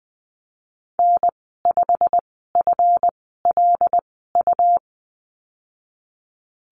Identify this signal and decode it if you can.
Morse code, heard as a keyed tone, about 20 wpm, spaced as for 15 wpm, 700 Hz. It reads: N5FLU
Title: 219 Lesson 35g - Number 0 - Call Signs 20wpm